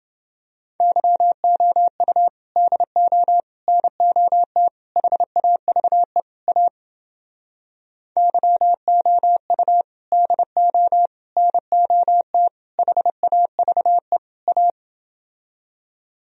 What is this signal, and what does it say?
Morse code, heard as a keyed tone, about 30 wpm, 700 Hz. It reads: YOU DO NOT HAVE A YOU DO NOT HAVE A